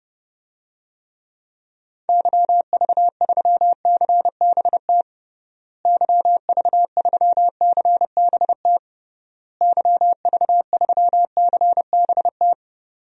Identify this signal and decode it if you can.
Morse code, heard as a keyed tone, about 30 wpm, 700 Hz. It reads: YV3CBT YV3CBT YV3CBT
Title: DXCC-100 Call Signs - 6 Characters - Repeated 3x - 30wpm